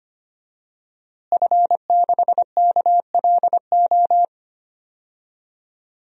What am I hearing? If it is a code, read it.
Morse code, heard as a keyed tone, about 25 wpm, 700 Hz. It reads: F6KLO
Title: DXCC-100 Call Signs - 5 Characters - Repeated 3x - 25wpm